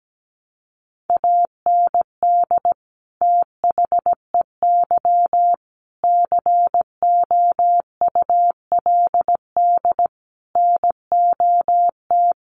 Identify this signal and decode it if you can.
Morse code, heard as a keyed tone, about 17 wpm, 700 Hz. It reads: AND THEY COULD NOT
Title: Sentences from Top 1100 Words 17wpm